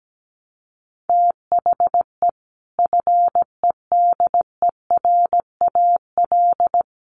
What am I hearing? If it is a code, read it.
Morse code, heard as a keyed tone, about 17 wpm, 700 Hz. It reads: THE FEDERAL
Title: Sentences from Top 1100 Words 17wpm